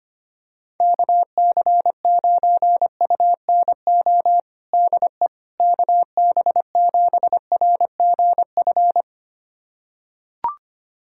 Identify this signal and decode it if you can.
Morse code, heard as a keyed tone, about 25 wpm, 700 Hz. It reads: KC9UNO DE KB7RGF
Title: USA Callsign DE Callsign 25wpm